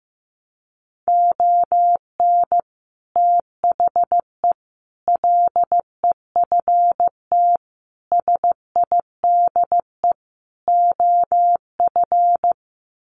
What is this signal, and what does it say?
Morse code, heard as a keyed tone, about 15 wpm, 700 Hz. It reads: ON THE LEFT SIDE OF
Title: Sets of 5 Words Encore2 15wpm